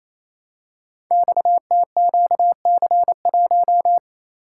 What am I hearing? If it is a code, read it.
Morse code, heard as a keyed tone, about 28 wpm, 700 Hz. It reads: XTQC1
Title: Code Groups - 5 Letter:Numbers - 28wpm